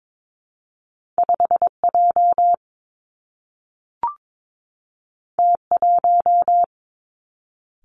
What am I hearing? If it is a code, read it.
Morse code, heard as a keyed tone, about 22 wpm, 700 Hz. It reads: 5 J T 1
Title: Two Challenging Letter-Number Combinations 22wpm